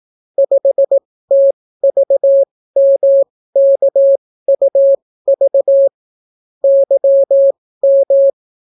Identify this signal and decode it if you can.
Morse code, heard as a keyed tone, about 18 wpm, spaced as for 14 wpm, 550 Hz. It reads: 5TVMKUV YM